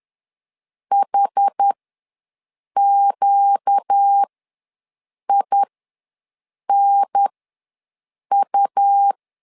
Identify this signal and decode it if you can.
Morse code, heard as a keyed tone, about 10 wpm, spaced as for 6 wpm, 800 Hz. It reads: HQINU
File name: LEZIONI / 4 Q1 ws 10wpm 2538